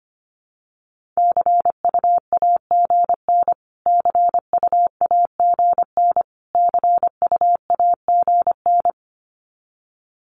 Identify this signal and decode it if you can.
Morse code, heard as a keyed tone, about 25 wpm, 700 Hz. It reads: CUAGN CUAGN CUAGN
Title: Abbreviations Repeated 3x - 25wpm